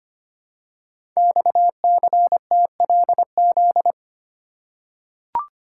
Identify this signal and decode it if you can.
Morse code, heard as a keyed tone, about 25 wpm, 700 Hz. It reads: XCTLZ
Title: Code Groups - 5 Letters - 25wpm